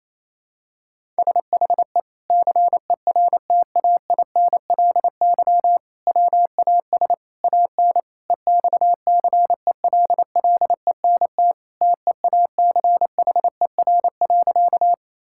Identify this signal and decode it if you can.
Morse code, heard as a keyed tone, about 28 wpm, 700 Hz. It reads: SHE CERTAINLY WAS AN EXCELLENT TEACHER.
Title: Sentences from Top 1000 Words - Review Part 6 - 28wpm